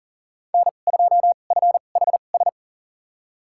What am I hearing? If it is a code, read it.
Morse code, heard as a keyed tone, about 37 wpm, spaced as for 28 wpm, 700 Hz. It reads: N2FHS
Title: USA Call Signs - 2x Character Spacing - 40wpm